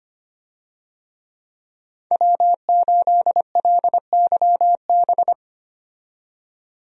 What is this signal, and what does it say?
Morse code, heard as a keyed tone, about 25 wpm, 700 Hz. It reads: W8LYB
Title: USA Call Signs - 5 Characters - 25wpm